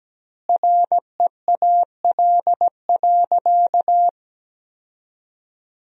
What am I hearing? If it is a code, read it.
Morse code, heard as a keyed tone, about 17 wpm, 700 Hz. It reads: REAL.
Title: 2x Word-Spacing - Sentences from Top 300 Words - 17wpm